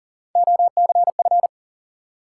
Morse code, heard as a keyed tone, about 41 wpm, 700 Hz. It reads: OCF
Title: Top 180 QSO Elements 40wpm